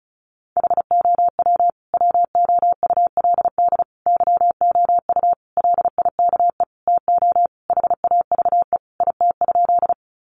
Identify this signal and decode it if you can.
Morse code, heard as a keyed tone, about 34 wpm, 700 Hz. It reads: HOW WOULD YOU LIKE TO HAVE IT?
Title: Sentences from Top 100 Words 35wpm